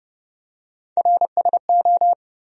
Morse code, heard as a keyed tone, about 30 wpm, 700 Hz. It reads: RSO
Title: USA Call Sign Postfixes - 3 Letters 30wpm